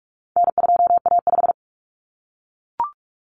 Morse code, heard as a keyed tone, about 44 wpm, 700 Hz. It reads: N2A5
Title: Common Cut Numbers - 4-Digit 45wpm